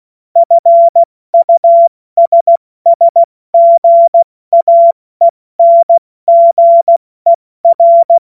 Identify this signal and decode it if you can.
Morse code, heard as a keyed tone, about 16 wpm, spaced as for 14 wpm, 680 Hz. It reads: FUSSGAENGER
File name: Witz-30-September-2025-70-F80